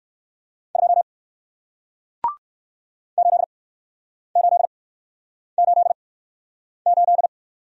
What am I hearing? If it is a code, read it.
Morse code, heard as a keyed tone, about 60 wpm, spaced as for 9 wpm, 700 Hz. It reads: 2 8888